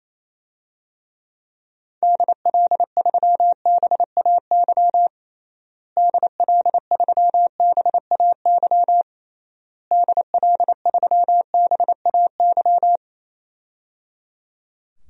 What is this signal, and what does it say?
Morse code, heard as a keyed tone, about 27 wpm, 700 Hz. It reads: DL3BAY DL3BAY DL3BAY
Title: DXCC-100 Call Signs - 6 Characters - Repeated 3x - 28wpm